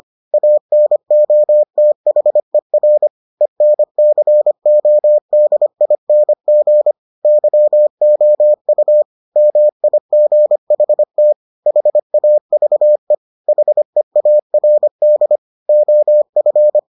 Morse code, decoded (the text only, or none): ANOTHER ENCODING YOU MIGHT HAVE HEARD OF